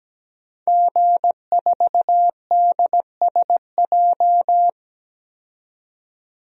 Morse code, heard as a keyed tone, about 17 wpm, 700 Hz. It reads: G4DSJ
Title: DXCC-100 Call Signs - 5 Characters - 17wpm